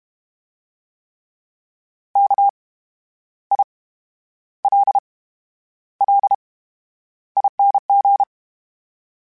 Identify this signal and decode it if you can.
Morse code, heard as a keyed tone, about 32 wpm, 800 Hz. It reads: K I L L ING
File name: Special INGs / Special INGs-mp3s / VerySlowToFaster Ing @30